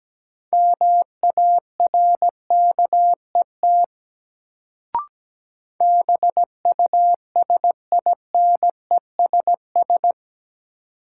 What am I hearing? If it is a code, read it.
Morse code, heard as a keyed tone, about 17 wpm, 700 Hz. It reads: MARKET BUSINESS
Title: Sentences from Top 600 Words - New Word Drill - 17wpm